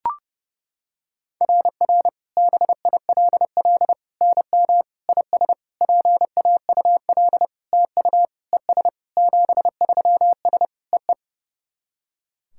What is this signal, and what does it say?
Morse code, heard as a keyed tone, about 32 wpm, 700 Hz. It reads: RR BILL NM IS PAUL TU ES 73S EE
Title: SOTA QSO Practice 30wpm